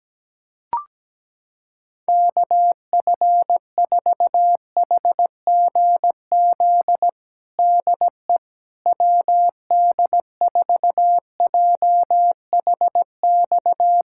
Morse code, heard as a keyed tone, about 17 wpm, 700 Hz. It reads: KF4HGZ DE WD4JHX
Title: USA Callsign DE Callsign - 6 Characters - 17wpm